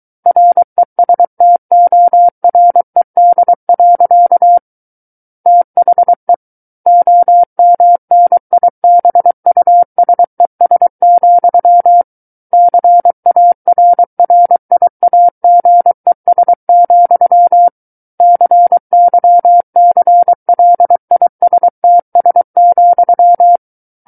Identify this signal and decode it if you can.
Morse code, heard as a keyed tone, about 23 wpm, 700 Hz. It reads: RESTORED. THE OMNIBUSES, CARRIAGES, CYCLISTS,
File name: Book One / 14-Chapter 14 - 23 WPM